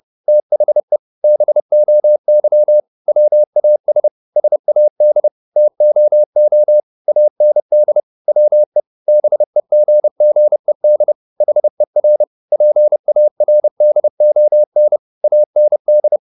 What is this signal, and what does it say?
Morse code, heard as a keyed tone, about 30 wpm, 600 Hz. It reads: THE BOY WAS SAD TOO AND WE BEGGED HER PARDON AND
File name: the old man and the sea / the old man and the sea 1 30wpm